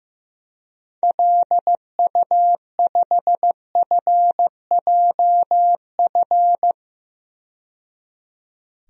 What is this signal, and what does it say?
Morse code, heard as a keyed tone, about 16 wpm, 700 Hz. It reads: LU5FJF
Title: DXCC-100 Call Signs - 6 Characters - 15wpm